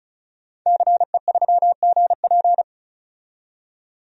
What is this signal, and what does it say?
Morse code, heard as a keyed tone, about 36 wpm, 700 Hz. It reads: CE3GP
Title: DXCC-100 Call Signs - 5 Characters - 35wpm